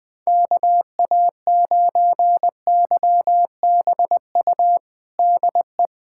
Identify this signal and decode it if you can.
Morse code, heard as a keyed tone, about 20 wpm, 700 Hz. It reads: KA9YBU DE
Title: USA Callsign DE Callsign 20wpm